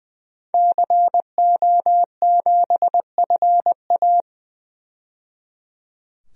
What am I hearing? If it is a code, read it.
Morse code, heard as a keyed tone, about 21 wpm, 700 Hz. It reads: CO7FA
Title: DXCC-100 Call Signs - 5 Characters - 20wpm